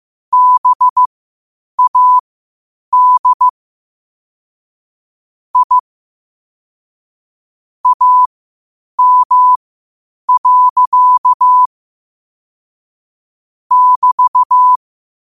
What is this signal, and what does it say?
Morse code, heard as a keyed tone, about 15 wpm, spaced as for 8 wpm, 1000 Hz. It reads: BAD I AM. <BT>
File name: qotdfiles / 8wpm